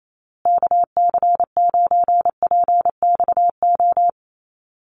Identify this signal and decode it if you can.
Morse code, heard as a keyed tone, about 28 wpm, 700 Hz. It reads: KC9PXO
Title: USA Call Signs Repeated 3x 28wpm